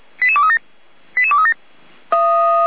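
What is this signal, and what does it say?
Five-tone selective call (ZVEI), heard as quick runs of five tones, tones of about 70 ms. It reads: 88237 88237
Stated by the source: ZVEI five-tone selective call received on SDR.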